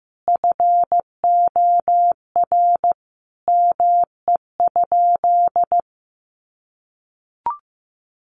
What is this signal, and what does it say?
Morse code, heard as a keyed tone, about 15 wpm, 700 Hz. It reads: FOR ME?
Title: Sentences from Top 1400 Words 15wpm